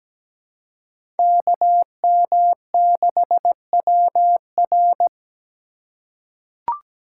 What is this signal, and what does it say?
Morse code, heard as a keyed tone, about 17 wpm, 700 Hz. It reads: KM6WR
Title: USA Call Signs - 5 Characters - 3x Character Spacing - 17wpm